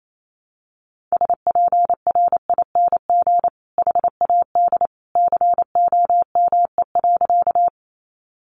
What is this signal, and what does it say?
Morse code, heard as a keyed tone, about 28 wpm, 700 Hz. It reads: SPRING HAD COME.
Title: Sentences from Top 1200 Words 28wpm